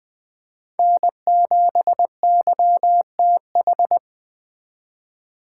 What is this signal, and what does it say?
Morse code, heard as a keyed tone, about 24 wpm, 700 Hz. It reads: N7YTH
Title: Speed Racing - USA 1x3 Call Signs 20wpm